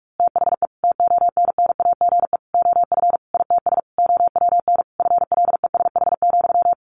Morse code, heard as a keyed tone, about 44 wpm, 700 Hz. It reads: THE TONNAGE OF ITS OWN FLESH,
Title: A Sound Of Thunder 45wpm